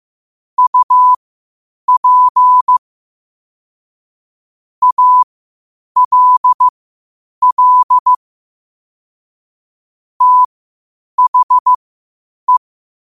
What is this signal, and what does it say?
Morse code, heard as a keyed tone, about 15 wpm, spaced as for 8 wpm, 1000 Hz. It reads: UP ALL THE
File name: qotdfiles / 8wpm